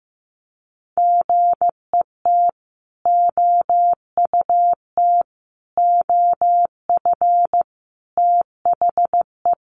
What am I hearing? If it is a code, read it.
Morse code, heard as a keyed tone, about 15 wpm, 700 Hz. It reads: GET OUT OF THE
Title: Sentences from Top 200 Words 15wpm